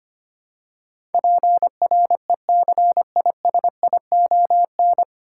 Morse code, heard as a keyed tone, about 25 wpm, 700 Hz. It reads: PRECISION
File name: practice / 25wpm / 9 Letter Words - First 3 Letters Given - 25wpm